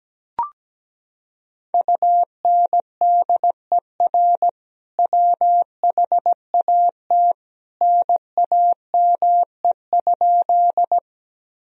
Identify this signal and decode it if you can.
Morse code, heard as a keyed tone, about 17 wpm, 700 Hz. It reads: UNDER WHAT NAME?
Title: Sentences from Top 1000 Words - Review Part 1 - 17wpm